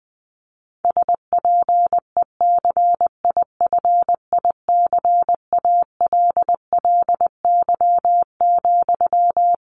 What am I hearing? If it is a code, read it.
Morse code, heard as a keyed tone, about 20 wpm, 700 Hz. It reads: SPECIFICALLY,
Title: Sentences from Top 1100 Words 20wpm